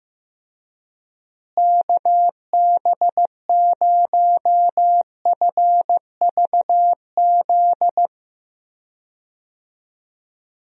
Morse code, heard as a keyed tone, about 15 wpm, 700 Hz. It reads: KB0FVZ
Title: USA Call Signs - 6 Characters - Repeated 3x - 15wpm